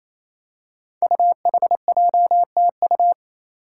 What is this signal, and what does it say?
Morse code, heard as a keyed tone, about 28 wpm, 700 Hz. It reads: UHJTU
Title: Five Challenging Letter-Number Combinations 28wpm